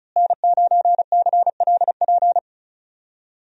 Morse code, heard as a keyed tone, about 36 wpm, 700 Hz. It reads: N9CLP
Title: USA Call Signs - 5 Characters - Repeated 2x - 35wpm